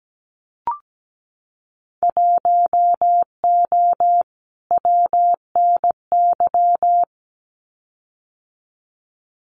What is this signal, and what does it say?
Morse code, heard as a keyed tone, about 17 wpm, 700 Hz. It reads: 1O WNY
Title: Winter Field Day Exchanges 17wpm